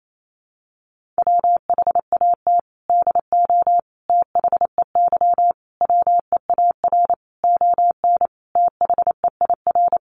Morse code, heard as a keyed tone, about 28 wpm, 700 Hz. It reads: WHAT DO THEY WEAR ON THEIR
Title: Sentences from Top 1500 Words 28wpm